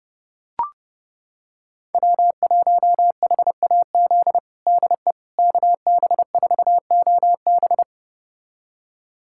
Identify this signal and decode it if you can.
Morse code, heard as a keyed tone, about 30 wpm, 700 Hz. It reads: W1HAZ DE KB4OB
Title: USA Callsign DE Callsign - 5 Characters - 30wpm